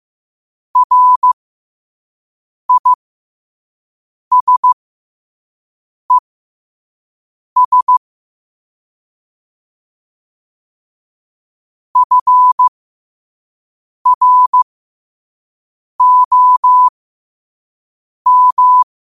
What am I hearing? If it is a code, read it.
Morse code, heard as a keyed tone, about 15 wpm, spaced as for 5 wpm, 1000 Hz. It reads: RISES FROM
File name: qotdfiles / 5wpm